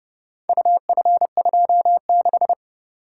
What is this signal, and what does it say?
Morse code, heard as a keyed tone, about 28 wpm, 700 Hz. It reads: UF26